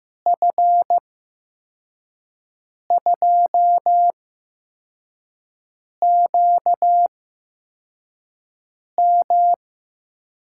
Morse code, heard as a keyed tone, about 15 wpm, spaced as for 4 wpm, 700 Hz. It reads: F2QM
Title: USA Call Signs - 5 Characters - 8x Character Spacing - 15wpm